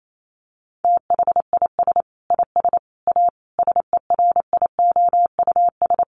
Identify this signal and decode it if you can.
Morse code, heard as a keyed tone, about 28 wpm, 700 Hz. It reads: THIS IS A SERIOUS
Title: Sentences from Top 1100 Words 28wpm